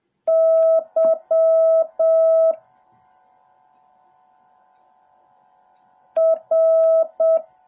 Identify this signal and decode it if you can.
Morse code, heard as a keyed tone, about 7 wpm, 650 Hz. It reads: Y R
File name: jh8cbh / 2021cw / cw08a